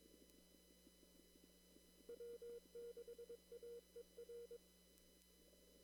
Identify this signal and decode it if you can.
Morse code, heard as a keyed tone, about 22 wpm, 470 Hz. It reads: W6AER
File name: CW / kt5j wpx cw-op-k5pi